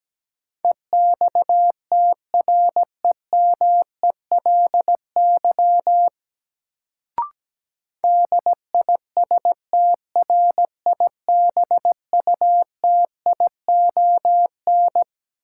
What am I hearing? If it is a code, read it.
Morse code, heard as a keyed tone, about 17 wpm, 700 Hz. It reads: EXTREMELY DISTRIBUTION
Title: Sentences from Top 1000 Words - Review Part 7 - New Word Drill - 17wpm